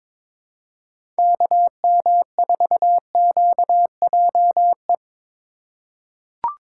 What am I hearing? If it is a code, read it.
Morse code, heard as a keyed tone, about 22 wpm, 700 Hz. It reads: KM4QJE
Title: USA Call Signs - 6 Characters - 4x Character Spacing - 22wpm